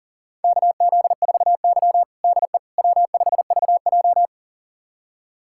Morse code, heard as a keyed tone, about 40 wpm, 700 Hz. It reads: KZ4Y DE W5VJ
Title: USA Callsign DE Callsign - 4 Characters - 40wpm